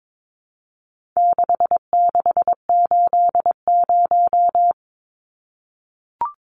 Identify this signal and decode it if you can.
Morse code, heard as a keyed tone, about 22 wpm, 700 Hz. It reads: 6680
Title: Numbers - 4-Digit 22wpm